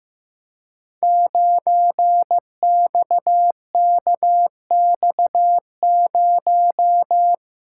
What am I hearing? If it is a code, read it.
Morse code, heard as a keyed tone, about 15 wpm, 700 Hz. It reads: 9XKX0
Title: Code Groups - 5 Letter:Numbers - 15wpm